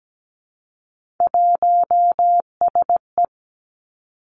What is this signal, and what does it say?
Morse code, heard as a keyed tone, about 17 wpm, 700 Hz. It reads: 1SE
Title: Three Challenging Letter-Number Combinations 17wpm